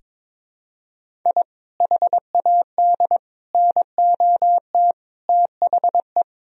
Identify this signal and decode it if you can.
Morse code, heard as a keyed tone, about 22 wpm, 700 Hz. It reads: I HAD NOT THE
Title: Sentences from Top 1000 Words - Review Part 1 - 22wpm